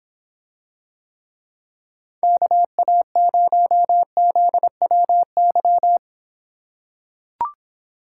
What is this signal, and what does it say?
Morse code, heard as a keyed tone, about 24 wpm, 700 Hz. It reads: KA0ZWY